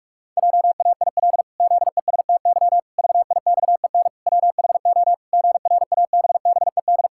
Morse code, heard as a keyed tone, about 46 wpm, 700 Hz. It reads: JAIL ZESTY VIXEN WHO GRABBED
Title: Pangram Challenge v2 - 45wpm